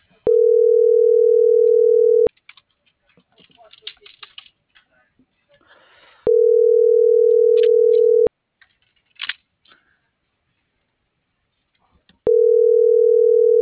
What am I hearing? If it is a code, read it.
Ringback tone (North America), 440 + 480 Hz.
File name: CENTRAL FRALSE / out erik2024sub - .85210